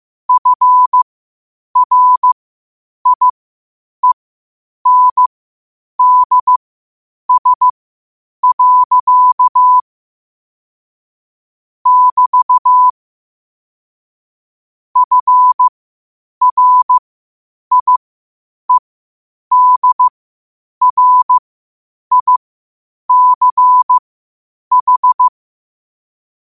Morse code, heard as a keyed tone, about 15 wpm, spaced as for 8 wpm, 1000 Hz. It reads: FRIENDS. <BT> FRIEDRICH
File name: qotdfiles / 8wpm